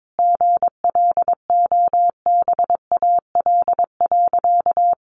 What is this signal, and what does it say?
Morse code, heard as a keyed tone, about 22 wpm, 700 Hz. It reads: GLOBAL.
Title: Sentences from Top 1200 Words 22wpm